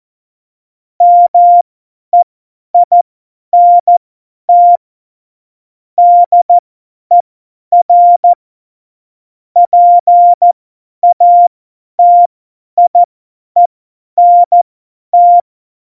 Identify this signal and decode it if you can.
Morse code, heard as a keyed tone, about 14 wpm, spaced as for 10 wpm, 700 Hz. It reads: MEINT DER PATIENT
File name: Morsekurs / Witze / Witz 22. September 2020, 50 F70